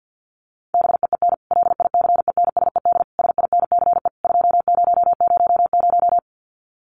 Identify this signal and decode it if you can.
Morse code, heard as a keyed tone, about 50 wpm, 700 Hz. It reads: BEEN LICENSED SINCE 2000
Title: Common QSO Phrases 50wpm